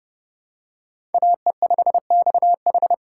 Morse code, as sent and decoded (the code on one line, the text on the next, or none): .- . ..... -..- ....
AE5XH